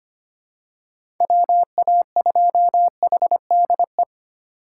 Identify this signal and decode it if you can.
Morse code, heard as a keyed tone, about 25 wpm, 700 Hz. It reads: WA2HDE
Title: USA Call Signs - 6 Characters - 4x Character Spacing - 25wpm